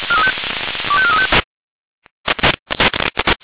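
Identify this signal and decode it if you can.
Five-tone selective call (ZVEI), heard as quick runs of five tones, tones of about 80 ms. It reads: 25435
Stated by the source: ZVEI five-tone call 25418 received by SDR.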